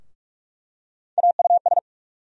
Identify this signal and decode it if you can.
Morse code, heard as a keyed tone, about 46 wpm, 700 Hz. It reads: AUS
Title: USA Call Sign Postfixes - 3 Letters 45wpm